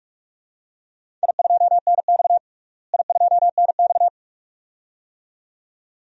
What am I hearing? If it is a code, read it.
Morse code, heard as a keyed tone, about 46 wpm, 700 Hz. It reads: I2NX I2NX
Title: DXCC-100 Call Signs - 4 Characters - Repeated 2x - 45wpm